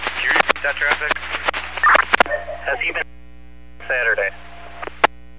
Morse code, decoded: none

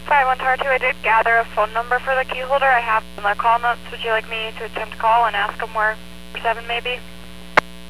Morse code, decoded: none

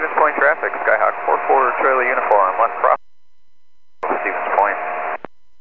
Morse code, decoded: none

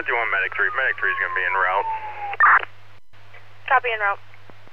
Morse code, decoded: none

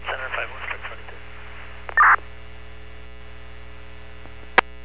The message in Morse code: none

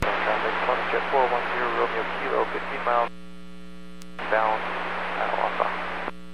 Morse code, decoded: none

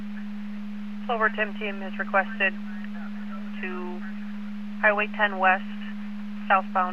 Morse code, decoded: none

